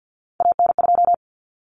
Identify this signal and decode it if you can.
Morse code, heard as a keyed tone, about 50 wpm, 700 Hz. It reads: AN2
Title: Common Cut Numbers - 3-Digit 50wpm